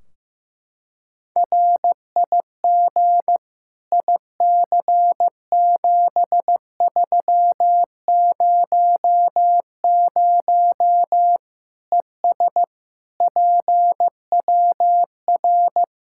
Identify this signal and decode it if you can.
Morse code, heard as a keyed tone, about 15 wpm, 700 Hz. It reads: RIG IC7300 ES PWR
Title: Practice QSO - Exchange 2 - Info - 15wpm